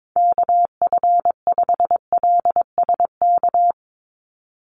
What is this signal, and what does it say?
Morse code, heard as a keyed tone, about 22 wpm, 700 Hz. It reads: KF5LSK
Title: USA Call Signs 22wpm